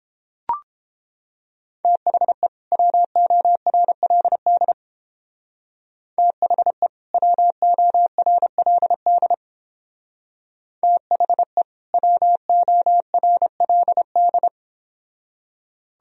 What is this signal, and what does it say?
Morse code, heard as a keyed tone, about 29 wpm, 700 Hz. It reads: THE WORLD THE WORLD THE WORLD